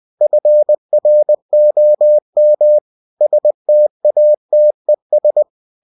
Morse code, heard as a keyed tone, about 20 wpm, 600 Hz. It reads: FROM STATES